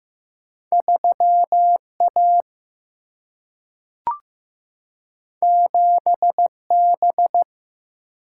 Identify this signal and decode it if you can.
Morse code, heard as a keyed tone, about 15 wpm, 700 Hz. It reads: 3 A 7 B
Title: Single-Letter Single-Number 15wpm